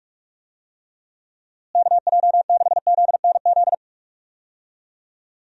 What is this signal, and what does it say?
Morse code, heard as a keyed tone, about 48 wpm, 700 Hz. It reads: KJ6ZNZ
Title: USA Call Signs - 6 Characters - Repeated 3x - 45wpm